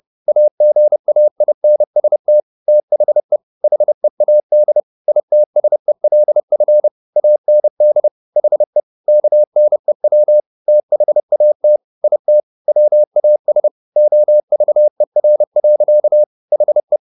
Morse code, heard as a keyed tone, about 30 wpm, 600 Hz. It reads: AGAINST THE HEAD ITSELF AND HE KNEW THAT IT WAS OVER. HE